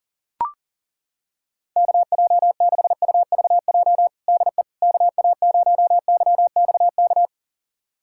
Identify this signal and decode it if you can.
Morse code, heard as a keyed tone, about 40 wpm, 700 Hz. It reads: KJ6UVJ DE KA0YXK
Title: USA Callsign DE Callsign - 6 Characters - 40wpm